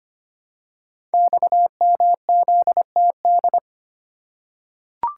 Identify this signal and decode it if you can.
Morse code, heard as a keyed tone, about 25 wpm, 700 Hz. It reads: XMZTD
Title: Code Groups - 5 Letter:Numbers - 25wpm